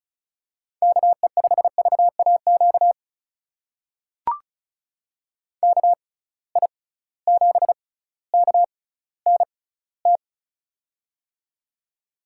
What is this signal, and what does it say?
Morse code, heard as a keyed tone, about 36 wpm, spaced as for 18 wpm, 700 Hz. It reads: KE5VAQ KI7KNT
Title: USA Call Signs - 6 Characters - 6x Character Spacing - 35wpm